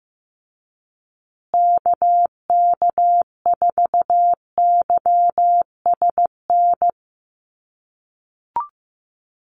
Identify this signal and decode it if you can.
Morse code, heard as a keyed tone, about 15 wpm, 700 Hz. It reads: KK4YSN
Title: USA Call Signs Repeated 2x 15wpm